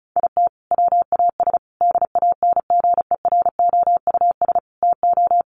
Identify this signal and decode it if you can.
Morse code, heard as a keyed tone, about 34 wpm, 700 Hz. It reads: IT WAS DANGEROUS TO
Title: Sentences from Top 1300 Words 35wpm